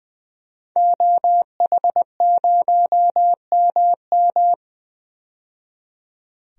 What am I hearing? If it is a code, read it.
Morse code, heard as a keyed tone, about 20 wpm, 700 Hz. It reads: OH0MM
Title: DXCC-100 Call Signs - 5 Characters - 20wpm